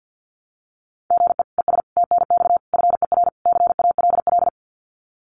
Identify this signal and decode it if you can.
Morse code, heard as a keyed tone, about 48 wpm, 700 Hz. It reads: GE ES TNX FER CALL
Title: Common QSO Phrases Encore 50wpm